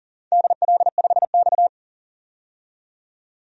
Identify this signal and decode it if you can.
Morse code, heard as a keyed tone, about 38 wpm, 700 Hz. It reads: DL5X
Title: DXCC-100 Call Signs - 4 Characters - 40wpm